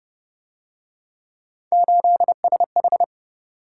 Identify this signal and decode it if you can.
Morse code, heard as a keyed tone, about 31 wpm, 700 Hz. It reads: 8SH
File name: Speed-Racing - Three Letter-Number Combinations - 25wpm